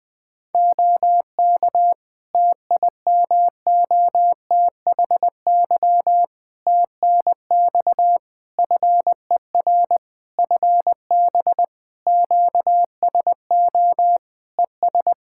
Morse code, heard as a keyed tone, about 20 wpm, 700 Hz. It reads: OK TIMOTHY TNX FER FB QSO ES
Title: K1USN SST QSO Practice 20wpm